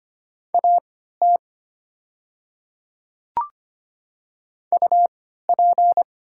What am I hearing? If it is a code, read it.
Morse code, heard as a keyed tone, about 25 wpm, spaced as for 14 wpm, 700 Hz. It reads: AT UP
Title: 3x Character Spacing - Top 100 Words - 25wpm